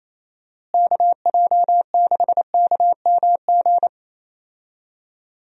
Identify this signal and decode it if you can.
Morse code, heard as a keyed tone, about 28 wpm, 700 Hz. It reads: KJ6KMG
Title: USA Call Signs - 6 Characters - Repeated 2x - 28wpm